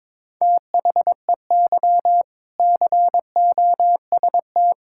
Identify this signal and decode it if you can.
Morse code, heard as a keyed tone, about 22 wpm, 700 Hz. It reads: THEY COST
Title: Sentences from Top 1000 Words - Review Part 5 - 22wpm